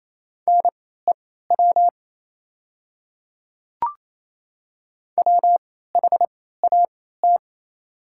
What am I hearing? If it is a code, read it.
Morse code, heard as a keyed tone, about 28 wpm, spaced as for 16 wpm, 700 Hz. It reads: NEW WHAT
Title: 3x Character Spacing - Top 300 Words - 28wpm